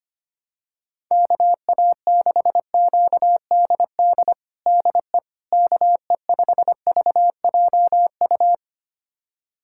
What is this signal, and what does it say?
Morse code, heard as a keyed tone, about 25 wpm, 700 Hz. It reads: KA6QDD DE KE5VJU
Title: USA Callsign DE Callsign 25wpm